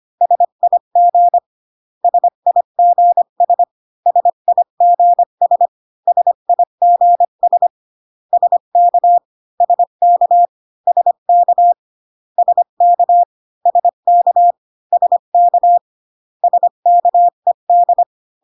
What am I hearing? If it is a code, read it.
Morse code, heard as a keyed tone, about 25 wpm, 700 Hz. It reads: SIG SIGS SIGS SIGS SK SK SK SK SK SK SKED